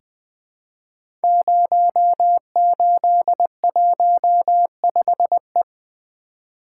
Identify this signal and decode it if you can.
Morse code, heard as a keyed tone, about 20 wpm, 700 Hz. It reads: 0815E
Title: Code Groups - 5 Letter:Numbers - 20wpm